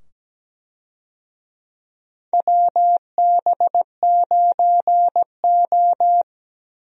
Morse code, heard as a keyed tone, about 17 wpm, 700 Hz. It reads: WB9O
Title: USA Call Signs - 4 Characters - 17wpm